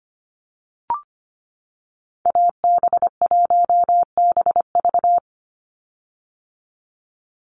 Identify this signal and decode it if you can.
Morse code, heard as a keyed tone, about 25 wpm, 700 Hz. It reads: AB1BV
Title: USA Call Signs 25wpm